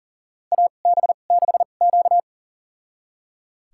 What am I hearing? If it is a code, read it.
Morse code, heard as a keyed tone, about 38 wpm, spaced as for 28 wpm, 690 Hz. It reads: AB6Q
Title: USA Call Signs - 4 Characters - 2x Character Spacing - 40wpm